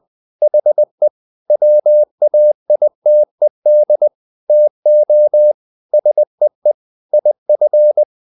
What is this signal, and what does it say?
Morse code, heard as a keyed tone, about 20 wpm, 600 Hz. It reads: HE WAITED TO SEE IF
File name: the old man and the sea part 2 20wpm